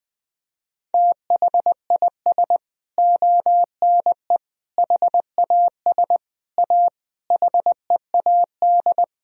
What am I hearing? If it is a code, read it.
Morse code, heard as a keyed tone, about 20 wpm, 700 Hz. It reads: THIS ONE HAS A HEAD
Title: Sentences from Top 1000 Words - Review Part 3 - 20wpm